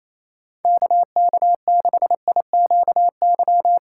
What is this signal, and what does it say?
Morse code, heard as a keyed tone, about 28 wpm, 700 Hz. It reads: KK6IQY
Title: USA Call Signs - 6 Characters - 3x Character Spacing - 28wpm